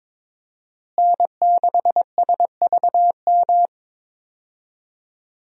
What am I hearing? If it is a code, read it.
Morse code, heard as a keyed tone, about 22 wpm, 700 Hz. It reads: N6SVM
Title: USA Call Signs - 5 Characters - Repeated 3x - 22wpm